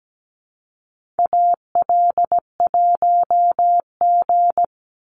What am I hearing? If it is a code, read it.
Morse code, heard as a keyed tone, about 17 wpm, 700 Hz. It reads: AL1G
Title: Call Signs Repeated 2x - 17wpm